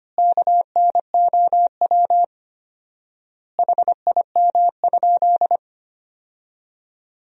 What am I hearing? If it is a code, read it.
Morse code, heard as a keyed tone, about 25 wpm, 700 Hz. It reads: KNOW HIM?
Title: 4x Word-Spacing - Sentences from Top 100 Words - 25wpm